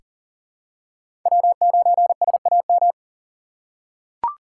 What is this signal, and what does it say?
Morse code, heard as a keyed tone, about 40 wpm, 700 Hz. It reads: W9SAM
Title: USA Call Signs - 5 Characters - 3x Character Spacing - 40wpm